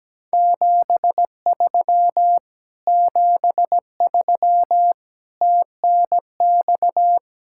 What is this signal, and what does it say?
Morse code, heard as a keyed tone, about 17 wpm, 700 Hz. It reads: 73 73 TNX
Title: Speed-Racing Common QSO Phrases - 17wpm